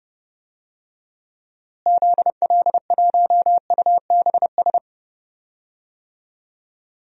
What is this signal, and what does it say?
Morse code, heard as a keyed tone, about 30 wpm, 700 Hz. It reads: ZL1UBS
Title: DXCC-100 Call Signs - 6 Characters - Repeated 3x - 30wpm